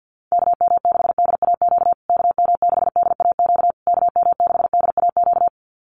Morse code, heard as a keyed tone, about 50 wpm, 700 Hz. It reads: KM6DAQ KM6DAQ KM6DAQ
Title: USA Call Signs Repeated 3x 50wpm